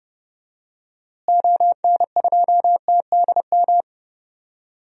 Morse code, heard as a keyed tone, about 30 wpm, 700 Hz. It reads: ON2TDM
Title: DXCC-100 Call Signs - 6 Characters - Repeated 2x - 30wpm